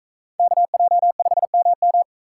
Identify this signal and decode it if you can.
Morse code, heard as a keyed tone, about 41 wpm, 700 Hz. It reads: KJ5MM